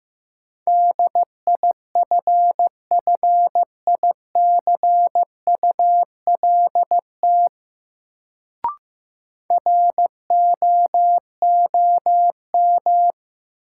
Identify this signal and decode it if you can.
Morse code, heard as a keyed tone, about 15 wpm, 700 Hz. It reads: DIFFICULT ROOM
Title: Sets of 4 Words - New Word Drill - 15wpm